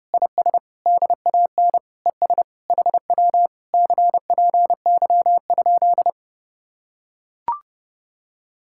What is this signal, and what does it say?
Morse code, heard as a keyed tone, about 30 wpm, 700 Hz. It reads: IS DAN ES HW CPY?